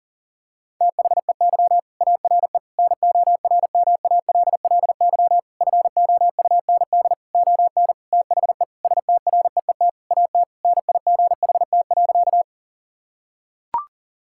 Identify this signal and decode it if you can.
Morse code, heard as a keyed tone, about 37 wpm, 700 Hz. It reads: THEY ARE NORMALLY FOUND ON THE STREET AT NIGHT.
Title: Sentences from Top 1000 Words - Review Part 6 - 40wpm